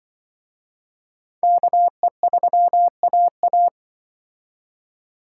Morse code, heard as a keyed tone, about 25 wpm, 700 Hz. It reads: KE3AA